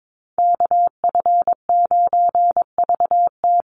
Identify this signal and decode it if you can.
Morse code, heard as a keyed tone, about 22 wpm, 700 Hz. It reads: KF9VT
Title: USA Call Signs 22wpm